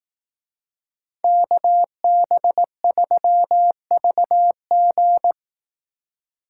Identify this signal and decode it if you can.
Morse code, heard as a keyed tone, about 18 wpm, 700 Hz. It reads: KB3VG
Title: Speed Racing - USA 2x2 Call Signs 15wpm